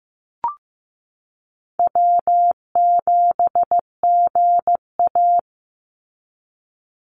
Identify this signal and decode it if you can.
Morse code, heard as a keyed tone, about 15 wpm, 700 Hz. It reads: W7GA
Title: NA SOTA Chaser Callsigns 15wpm